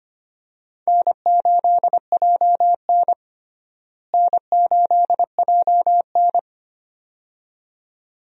Morse code, heard as a keyed tone, about 25 wpm, 700 Hz. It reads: N8JN N8JN
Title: USA Call Signs - 4 Characters - Repeated 2x - 25wpm